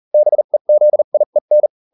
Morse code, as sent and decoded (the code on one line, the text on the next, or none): -... --.. .. -.
BEZIEN